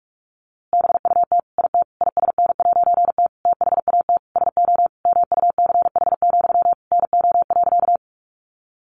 Morse code, heard as a keyed tone, about 44 wpm, 700 Hz. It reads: BUT IT ISN'T THAT SO MUCH, NO.
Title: A Sound Of Thunder 45wpm